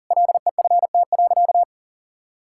Morse code, heard as a keyed tone, about 40 wpm, 700 Hz. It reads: LEFT.
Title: Sentences from Top 1000 Words - Review Part 5 - 40wpm